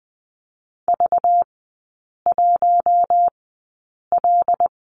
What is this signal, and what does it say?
Morse code, heard as a keyed tone, about 20 wpm, spaced as for 8 wpm, 690 Hz. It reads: V1L
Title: 191 Lesson 31e - Letter v - Three Characters 20wpm